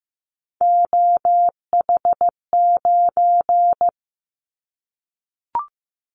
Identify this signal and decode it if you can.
Morse code, heard as a keyed tone, about 15 wpm, 700 Hz. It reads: OH9
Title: Callsign Prefixes for Top 50 DXCC - 15wpm